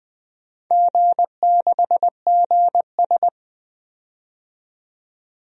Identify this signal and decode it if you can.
Morse code, heard as a keyed tone, about 20 wpm, 700 Hz. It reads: G6GS
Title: DXCC-100 Call Signs - 4 Characters - 20wpm